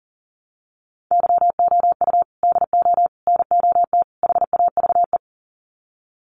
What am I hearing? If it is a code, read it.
Morse code, heard as a keyed tone, about 38 wpm, 700 Hz. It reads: YOU DO NOT HAVE
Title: Sets of 4 Words 40wpm